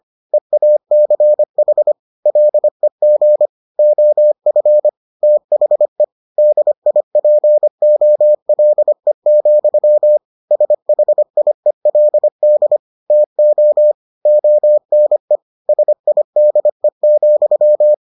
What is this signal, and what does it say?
Morse code, heard as a keyed tone, about 25 wpm, 600 Hz. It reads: EACH LEG OF THE DIPOLE, SHIELD TO ONE SIDE,